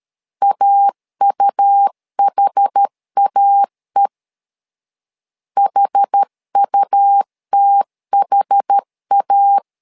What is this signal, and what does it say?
Morse code, heard as a keyed tone, about 13 wpm, 800 Hz. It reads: AUHAE HUTHA
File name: LEZIONI / 2 UHDN / 2 UHD 12wpm 1540